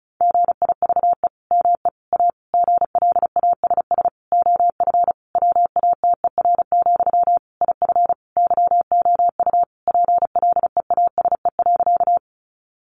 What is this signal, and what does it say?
Morse code, heard as a keyed tone, about 34 wpm, 700 Hz. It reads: GIVE ME A GLASS OF WATER, IF YOU PLEASE.
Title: Sentences from Top 1500 Words 35wpm